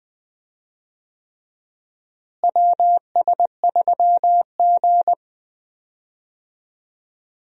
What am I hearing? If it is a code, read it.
Morse code, heard as a keyed tone, about 20 wpm, 700 Hz. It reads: WS3G
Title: USA Call Signs - 4 Characters - Repeated 3x - 20wpm